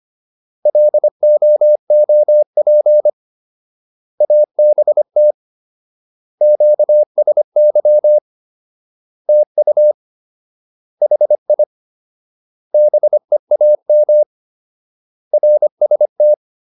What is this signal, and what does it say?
Morse code, heard as a keyed tone, about 25 wpm, 600 Hz. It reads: LOOP ABT QSY TU HI BEAM RST